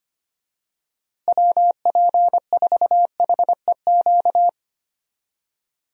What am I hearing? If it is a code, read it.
Morse code, heard as a keyed tone, about 25 wpm, 700 Hz. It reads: WP4HEQ
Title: USA Call Signs - 6 Characters - Repeated 2x - 25wpm